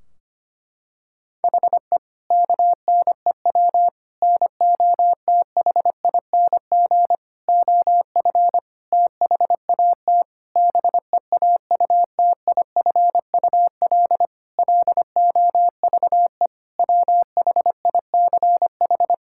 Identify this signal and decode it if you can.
Morse code, heard as a keyed tone, about 25 wpm, 700 Hz. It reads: HE KNEW NOTHING OF THAT BEAUTIFUL LOVE WHICH
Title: Sentences from Top 1600 Words 25wpm